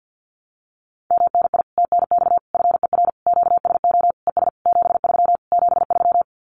Morse code, heard as a keyed tone, about 50 wpm, 700 Hz. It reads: MNI TNX FER QSO ES 73 73
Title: Common QSO Phrases Encore 50wpm